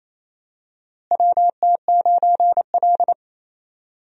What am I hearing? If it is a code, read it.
Morse code, heard as a keyed tone, about 28 wpm, 700 Hz. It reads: WT9L
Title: USA Call Signs - 4 Characters - Repeated 3x - 28wpm